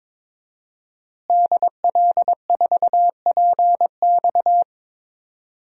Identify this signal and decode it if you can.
Morse code, heard as a keyed tone, about 22 wpm, 700 Hz. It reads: DL4PX
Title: DXCC-100 Call Signs - 5 Characters - Repeated 3x - 22wpm